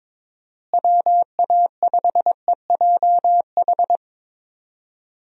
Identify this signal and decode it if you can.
Morse code, heard as a keyed tone, about 22 wpm, 700 Hz. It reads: WA5EJH
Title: USA Call Signs - 6 Characters - Repeated 2x - 22wpm